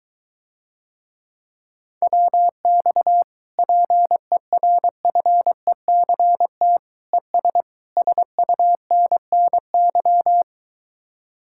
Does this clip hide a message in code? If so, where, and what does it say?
Yes: Morse code from 2.0 to 10.4 s, heard as a keyed tone, about 23 wpm, 700 Hz. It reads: WX PERFECT ES SUNNY